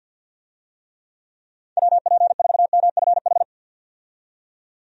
Morse code, heard as a keyed tone, about 46 wpm, 700 Hz. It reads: WP4MFH
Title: USA Call Signs - 6 Characters - 50wpm